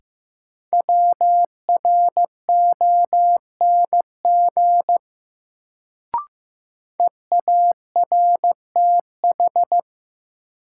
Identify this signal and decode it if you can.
Morse code, heard as a keyed tone, about 15 wpm, 700 Hz. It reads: WRONG EARTH
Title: Sets of 5 Words Encore - New Word Drill - 15wpm